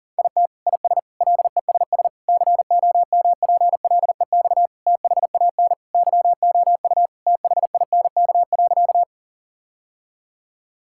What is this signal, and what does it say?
Morse code, heard as a keyed tone, about 40 wpm, 700 Hz. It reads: IT IS LESS COMPLEX THAN YOU THINK.
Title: Sentences from Top 1000 Words - Review Part 6 - 40wpm